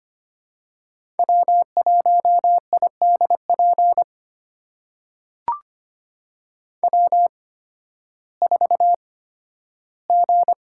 Morse code, heard as a keyed tone, about 25 wpm, spaced as for 13 wpm, 700 Hz. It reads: W1IDP W4G
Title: USA Call Signs - 5 Characters - 8x Character Spacing - 25wpm